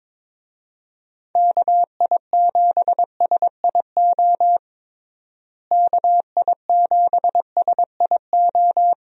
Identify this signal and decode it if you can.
Morse code, heard as a keyed tone, about 22 wpm, 700 Hz. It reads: KI7SIO KI7SIO
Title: USA Call Signs - 6 Characters - Repeated 2x - 22wpm